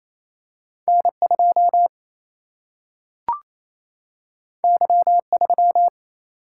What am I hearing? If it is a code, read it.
Morse code, heard as a keyed tone, about 28 wpm, 700 Hz. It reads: N2 Y3
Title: Single-Letter Single-Number 28wpm